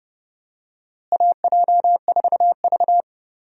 Morse code, heard as a keyed tone, about 30 wpm, 700 Hz. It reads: AJ4V
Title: USA Call Signs - 4 Characters - Repeated 3x - 30wpm